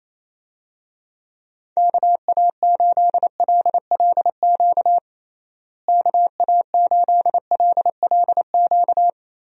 Morse code, heard as a keyed tone, about 28 wpm, 700 Hz. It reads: KA8LLQ KA8LLQ
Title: USA Call Signs - 6 Characters - Repeated 3x - 28wpm